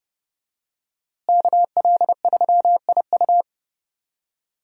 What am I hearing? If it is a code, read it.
Morse code, heard as a keyed tone, about 30 wpm, 700 Hz. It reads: KL3IU
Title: USA Call Signs - 5 Characters - Repeated 3x - 30wpm